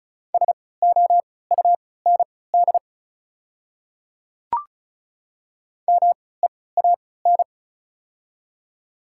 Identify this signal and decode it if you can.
Morse code, heard as a keyed tone, about 36 wpm, spaced as for 20 wpm, 690 Hz. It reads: SOUND MEAN
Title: 3x Character Spacing - Top 300 Words - 35wpm